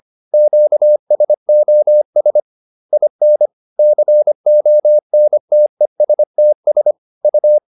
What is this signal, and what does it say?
Morse code, heard as a keyed tone, about 25 wpm, 600 Hz. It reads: QSOS IN CONTESTS U